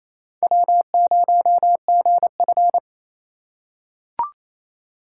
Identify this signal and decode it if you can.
Morse code, heard as a keyed tone, about 28 wpm, 700 Hz. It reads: W0GF
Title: USA Call Signs - 4x Character Spacing - 28wpm